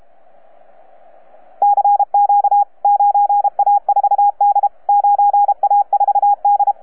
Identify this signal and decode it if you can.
Morse code, heard as a keyed tone, about 32 wpm, 810 Hz, with radio noise. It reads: CQ 9A4D 9A4D
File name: EF8M-WAEDC-CW-2010-20-CW